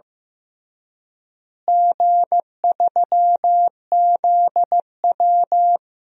Morse code, heard as a keyed tone, about 15 wpm, 700 Hz. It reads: G3ZW
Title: DXCC-100 Call Signs - 4 Characters - Repeated 3x - 15wpm